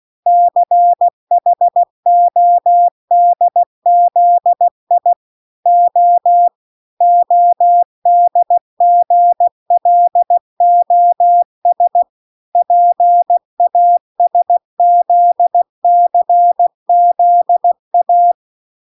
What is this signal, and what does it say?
Morse code, heard as a keyed tone, about 16 wpm, 700 Hz. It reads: CHODZI O ODGLOS PASZCZA